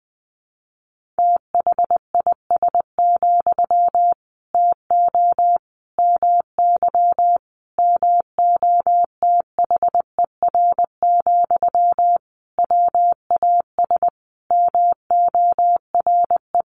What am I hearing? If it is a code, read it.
Morse code, heard as a keyed tone, about 20 wpm, 700 Hz. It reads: THIS, TO MY MOTHER, WAS MORE
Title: Sentences from Top 1500 Words 20wpm